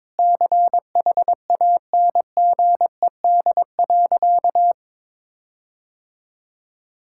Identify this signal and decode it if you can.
Morse code, heard as a keyed tone, about 22 wpm, 700 Hz. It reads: CHANGED.
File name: practice / 22wpm / Sentences with 4-Word Phrases - 22wpm